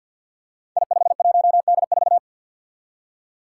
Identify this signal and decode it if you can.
Morse code, heard as a keyed tone, about 48 wpm, 700 Hz. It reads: I51D4
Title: Code Groups - 5 Letter:Numbers - 50wpm